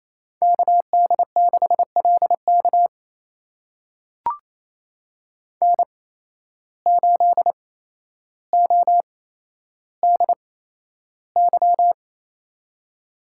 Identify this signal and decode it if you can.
Morse code, heard as a keyed tone, about 28 wpm, spaced as for 12 wpm, 700 Hz. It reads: KD6LK N8ODY
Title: USA Call Signs - 5 Characters - 8x Character Spacing - 28wpm